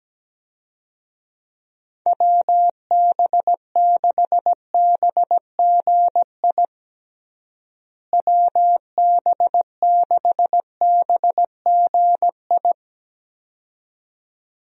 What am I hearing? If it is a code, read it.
Morse code, heard as a keyed tone, about 17 wpm, 700 Hz. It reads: WB6BGI WB6BGI
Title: USA Call Signs - 6 Characters - Repeated 2x - 17wpm